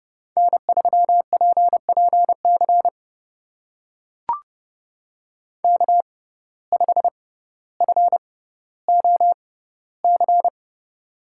Morse code, heard as a keyed tone, about 30 wpm, spaced as for 15 wpm, 700 Hz. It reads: N3PPC K5FOC
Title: USA Call Signs - 5 Characters - 6x Character Spacing - 30wpm